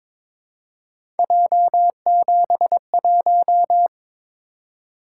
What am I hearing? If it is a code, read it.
Morse code, heard as a keyed tone, about 22 wpm, 700 Hz. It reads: J71
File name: practice / 22wpm / Three Letter-Number Combinations - 22wpm